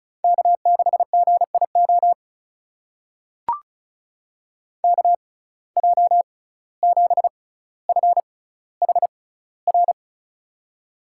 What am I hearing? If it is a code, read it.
Morse code, heard as a keyed tone, about 36 wpm, spaced as for 17 wpm, 700 Hz. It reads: K6GIO KJ7FHR
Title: USA Call Signs - 6x Character Spacing - 35wpm